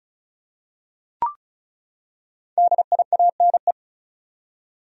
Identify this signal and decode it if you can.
Morse code, heard as a keyed tone, about 36 wpm, 700 Hz. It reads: DIANE
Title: USA Common Names - 35wpm